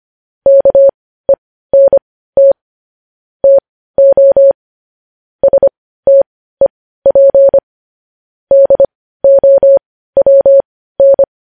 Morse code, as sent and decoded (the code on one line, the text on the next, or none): -.- . -. - / - --- / ... - . .--. / -.. --- .-- -.
KENT TO STEP DOWN